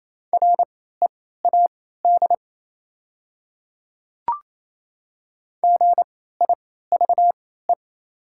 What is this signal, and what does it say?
Morse code, heard as a keyed tone, about 28 wpm, spaced as for 16 wpm, 700 Hz. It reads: READ GIVE
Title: 3x Character Spacing - Top 300 Words - 28wpm